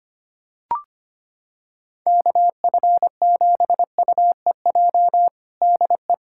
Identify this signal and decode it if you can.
Morse code, heard as a keyed tone, about 25 wpm, 700 Hz. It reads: KF7UEJ DE
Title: Practice QSO - Exchange 4 - Ending - 25wpm